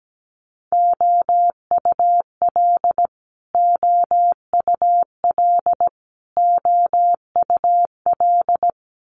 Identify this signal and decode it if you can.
Morse code, heard as a keyed tone, about 17 wpm, 700 Hz. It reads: OUL OUL OUL
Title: Multi-letter Phonemes Repeated 3x 17wpm